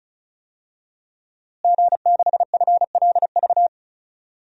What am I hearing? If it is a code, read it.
Morse code, heard as a keyed tone, about 34 wpm, 700 Hz. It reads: G6FLV
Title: DXCC-100 Call Signs - 5 Characters - Repeated 3x - 35wpm